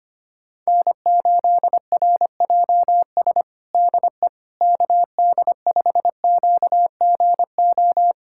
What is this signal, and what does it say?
Morse code, heard as a keyed tone, about 25 wpm, 700 Hz. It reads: N8RJS DE KD5QGO
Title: USA Callsign DE Callsign 25wpm